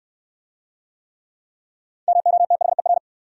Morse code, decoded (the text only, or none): MOTHER